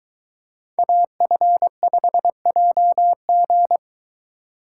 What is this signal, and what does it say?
Morse code, heard as a keyed tone, about 22 wpm, 700 Hz. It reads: AF5JG